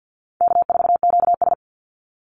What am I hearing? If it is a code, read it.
Morse code, heard as a keyed tone, about 50 wpm, 700 Hz. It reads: K4QS
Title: NA SOTA Activator Callsigns 50wpm